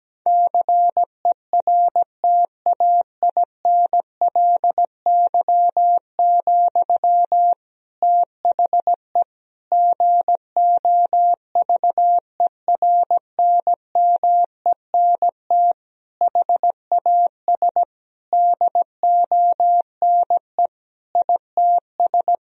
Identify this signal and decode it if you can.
Morse code, heard as a keyed tone, about 17 wpm, 700 Hz. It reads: CERTAINLY, THE GOVERNMENT HAS DONE ITS
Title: Sentences from Top 1000 Words - Review Part 3 - 17wpm